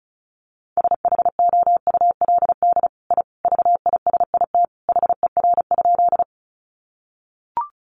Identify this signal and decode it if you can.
Morse code, heard as a keyed tone, about 34 wpm, 700 Hz. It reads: SHOULD I VISIT HER?
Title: Sentences from Top 900 Words 35wpm